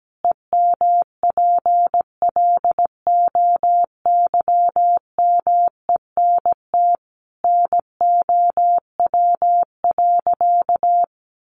Morse code, heard as a keyed tone, about 17 wpm, 700 Hz. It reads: EMPLOYMENT NOW.
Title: Sentences from Top 1500 Words 17wpm